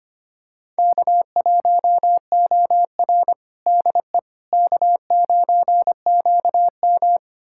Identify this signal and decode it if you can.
Morse code, heard as a keyed tone, about 25 wpm, 700 Hz. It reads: K1OR DE K9QM
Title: USA Callsign DE Callsign - 4 Characters - 25wpm